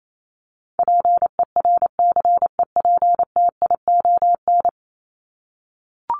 Morse code, heard as a keyed tone, about 28 wpm, 700 Hz. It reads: PERCEPTION
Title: Top 1751-2000 Words 28wpm